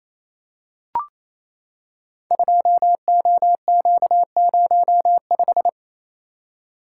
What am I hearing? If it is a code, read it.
Morse code, heard as a keyed tone, about 27 wpm, 700 Hz. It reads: 2OQ05
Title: Code Groups - 5 Letter:Numbers - 28wpm